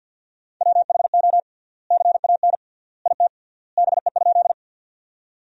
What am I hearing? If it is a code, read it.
Morse code, heard as a keyed tone, about 47 wpm, 700 Hz. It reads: WHO CAN IT BE?
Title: 3x Word-Spacing - Sentences from Top 100 Words - 50wpm